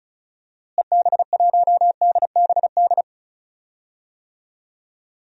Morse code, heard as a keyed tone, about 34 wpm, 700 Hz. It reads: EB1DBD
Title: DXCC-100 Call Signs - 6 Characters - Repeated 3x - 35wpm